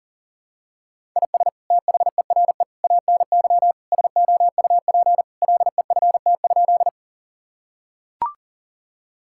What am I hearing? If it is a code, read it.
Morse code, heard as a keyed tone, about 40 wpm, 700 Hz. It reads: IS THERE ANY SOUP LEFT?
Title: Sentences from Top 1000 Words - Review Part 6 - 40wpm